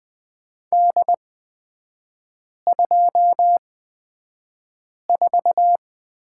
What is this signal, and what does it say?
Morse code, heard as a keyed tone, about 20 wpm, spaced as for 5 wpm, 700 Hz. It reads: D24
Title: 243 Lesson 39e - Letter z - Three Characters 20wpm